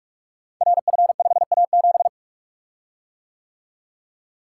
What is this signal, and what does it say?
Morse code, heard as a keyed tone, about 44 wpm, 690 Hz. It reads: RF5A7
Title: Code Groups - 5 Letter:Numbers - 45wpm